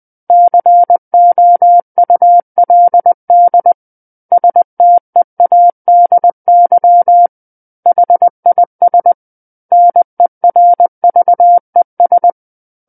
Morse code, heard as a keyed tone, about 20 wpm, 700 Hz. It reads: COULD STEADY HIS NERVES
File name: 20 WPM